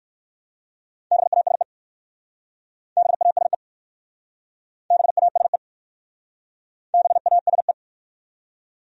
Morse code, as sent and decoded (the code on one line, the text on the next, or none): -... .- ... / -... .- ... / -... .- ... / -... .- ... .
BASE BASE BASE BASE